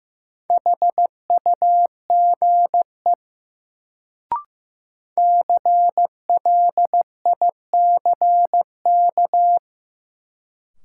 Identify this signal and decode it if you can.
Morse code, heard as a keyed tone, about 15 wpm, 700 Hz. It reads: HUGE CLICK
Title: Sentences from Top 1100 Words - New Word Drill - 15wpm